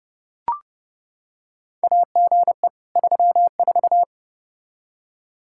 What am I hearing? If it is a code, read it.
Morse code, heard as a keyed tone, about 30 wpm, 700 Hz. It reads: AGE 34
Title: K1USN SST QSO Practice 30wpm